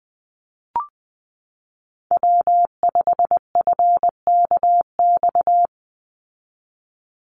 Morse code, heard as a keyed tone, about 20 wpm, 700 Hz. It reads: W5FKX
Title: Call Signs Repeated 3x - 20wpm